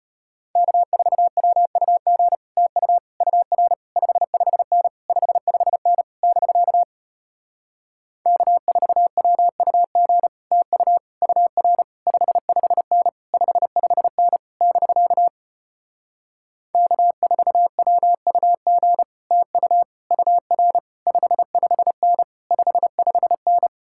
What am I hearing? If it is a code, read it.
Morse code, heard as a keyed tone, about 33 wpm, 700 Hz. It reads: K4WUG TU UR 55N 55N <BK> K4WUG TU UR 55N 55N <BK> K4WUG TU UR 55N 55N